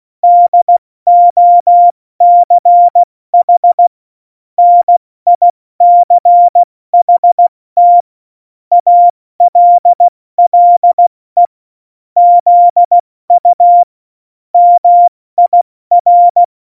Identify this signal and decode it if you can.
Morse code, heard as a keyed tone, about 16 wpm, 700 Hz. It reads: DOCH NICHT ALLE ZU MIR
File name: Morsekurs / Witze / Witz 06. Februar 2018, 70 F80